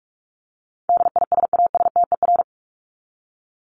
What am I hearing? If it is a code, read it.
Morse code, heard as a keyed tone, about 44 wpm, 700 Hz. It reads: DISASTER
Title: Top 1501-1750 Words 45wpm